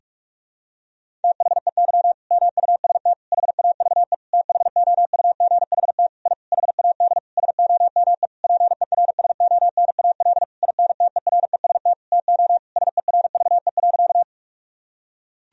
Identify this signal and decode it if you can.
Morse code, heard as a keyed tone, about 41 wpm, 700 Hz. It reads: THEY MUST HAVE THOUGHT I HAD SOME PERSONAL INTEREST TO SERVE.
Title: Sentences from Top 1000 Words - Review Part 4 - 45wpm